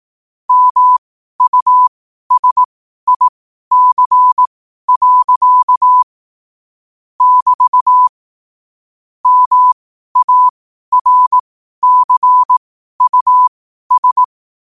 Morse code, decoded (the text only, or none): MUSIC. <BT> MARCUS